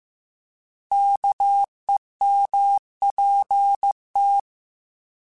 Morse code, heard as a keyed tone, about 15 wpm, 800 Hz. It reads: KEMPT
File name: Alphabetical / Alphabetical mp3s / Ks @13